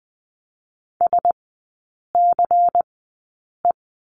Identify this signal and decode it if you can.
Morse code, heard as a keyed tone, about 20 wpm, 700 Hz. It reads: S C E
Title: 114 Lesson 19e - Letter w - Three Characters 20wpm